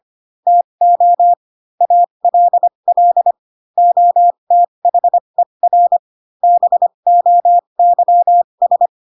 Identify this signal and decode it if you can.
Morse code, heard as a keyed tone, about 25 wpm, spaced as for 22 wpm, 700 Hz. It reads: TO ALL OTHER BOYS
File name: Pinocchio / Pinocchio ch4 18 25 ik1hge